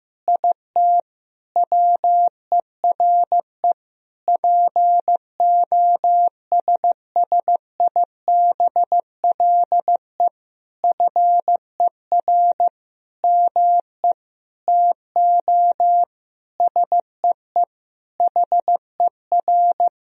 Morse code, heard as a keyed tone, about 15 wpm, 700 Hz. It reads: IT WERE POSSIBLE FER ME TO SEE HER
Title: Sentences from Top 500 Words and Abbreviations 15wpm